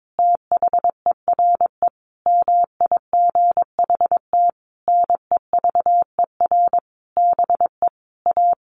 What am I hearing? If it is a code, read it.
Morse code, heard as a keyed tone, about 22 wpm, 700 Hz. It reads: THERE MIGHT NEVER BE A
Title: A Sound Of Thunder 22wpm